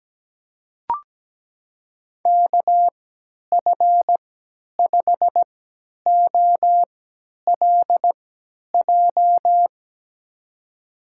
Morse code, heard as a keyed tone, about 17 wpm, spaced as for 10 wpm, 700 Hz. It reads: KF5OLJ
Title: USA Call Signs - 6 Characters - 3x Character Spacing - 17wpm